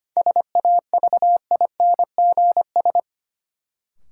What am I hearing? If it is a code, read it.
Morse code, heard as a keyed tone, about 25 wpm, 700 Hz. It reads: SAVINGS
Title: Sentences from Top 1200 Words - New Word Drill Alt - 25wpm